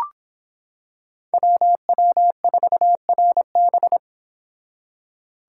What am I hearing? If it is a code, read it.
Morse code, heard as a keyed tone, about 25 wpm, 700 Hz. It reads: WW4RB